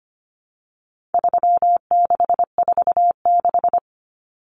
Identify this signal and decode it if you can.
Morse code, heard as a keyed tone, about 25 wpm, 700 Hz. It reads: 3646
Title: Common Cut Numbers - 4-Digit 25wpm